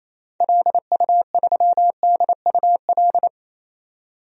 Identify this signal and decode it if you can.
Morse code, heard as a keyed tone, about 27 wpm, 700 Hz. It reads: LU3DUL
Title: DXCC-100 Call Signs - 6 Characters - 28wpm